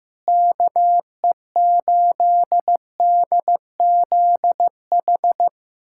Morse code, heard as a keyed tone, about 15 wpm, 700 Hz. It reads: KE8DZH
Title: USA Callsign DE Callsign 15wpm